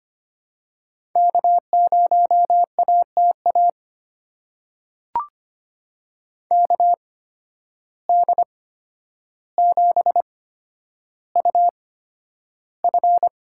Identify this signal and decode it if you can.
Morse code, heard as a keyed tone, about 25 wpm, spaced as for 11 wpm, 700 Hz. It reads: K0ATA KD7UF
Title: USA Call Signs - 8x Character Spacing - 25wpm